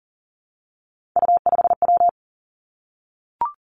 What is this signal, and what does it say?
Morse code, heard as a keyed tone, about 38 wpm, 700 Hz. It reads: U5W
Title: Three Challenging Letter-Number Combinations 40wpm